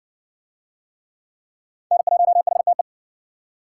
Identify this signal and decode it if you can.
Morse code, heard as a keyed tone, about 54 wpm, 700 Hz. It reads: N1HTE